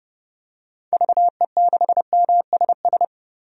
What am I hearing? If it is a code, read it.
Morse code, heard as a keyed tone, about 29 wpm, 700 Hz. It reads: VE6MSS
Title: DXCC-100 Call Signs - 6 Characters - 30wpm